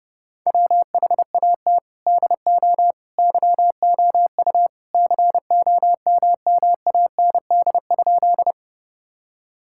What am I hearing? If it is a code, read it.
Morse code, heard as a keyed tone, about 30 wpm, 700 Hz. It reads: WHAT DO YOU COMMAND?
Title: Sentences from Top 1600 Words 30wpm